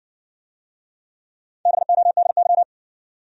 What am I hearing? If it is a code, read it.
Morse code, heard as a keyed tone, about 58 wpm, 700 Hz. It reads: BODY